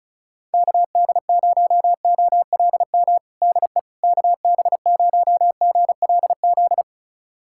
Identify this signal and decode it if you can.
Morse code, heard as a keyed tone, about 36 wpm, 700 Hz. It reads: KD0OLM DE KB0GLZ
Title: USA Callsign DE Callsign - 6 Characters - 35wpm